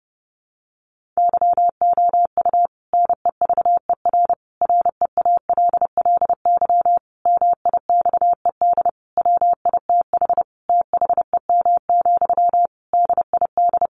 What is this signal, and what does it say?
Morse code, heard as a keyed tone, about 30 wpm, 700 Hz. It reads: YOU NEVER REALLY MIXED WITH THEM, DID
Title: Sentences from Top 1500 Words 30wpm